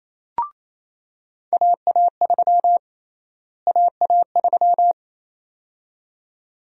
Morse code, heard as a keyed tone, about 28 wpm, 700 Hz. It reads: AA3 AA3
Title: USA Call Sign Prefixes - Repeated 2x - 28wpm